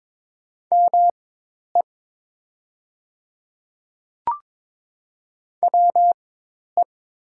Morse code, heard as a keyed tone, about 22 wpm, spaced as for 10 wpm, 700 Hz. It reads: ME WE
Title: 4x Character Spacing - Top 200 Words - 22wpm